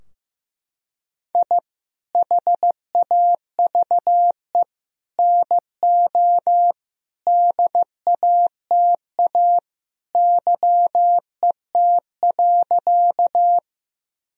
Morse code, heard as a keyed tone, about 15 wpm, 700 Hz. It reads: I HAVE NO DATA YET.
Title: Sentences from Top 1000 Words - Review Part 3 - 15wpm